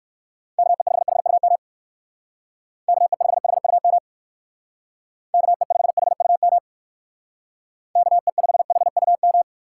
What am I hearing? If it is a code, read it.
Morse code, heard as a keyed tone, about 56 wpm, 700 Hz. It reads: KE5HUM KE5HUM KE5HUM KE5HUM
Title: Speed Racing - USA 2x3 Call Signs 45wpm